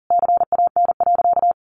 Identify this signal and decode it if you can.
Morse code, heard as a keyed tone, about 40 wpm, 700 Hz. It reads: CAN.
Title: Sentences from Top 1400 Words 40wpm